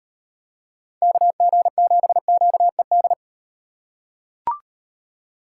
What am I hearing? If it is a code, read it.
Morse code, heard as a keyed tone, about 38 wpm, 700 Hz. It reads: KG7QED